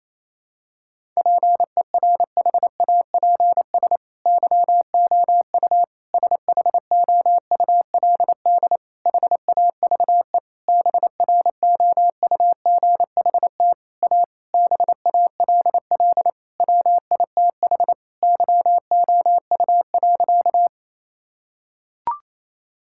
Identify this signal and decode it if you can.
Morse code, heard as a keyed tone, about 28 wpm, 700 Hz. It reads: PERHAPS YOU SHOULD HAVE BROUGHT A BALL WITH YOU.
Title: Sentences from Top 1000 Words - Review Part 6 - 28wpm